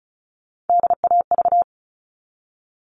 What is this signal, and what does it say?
Morse code, heard as a keyed tone, about 34 wpm, 700 Hz. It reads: DAV
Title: All Cut Numbers - 3-Digit 35wpm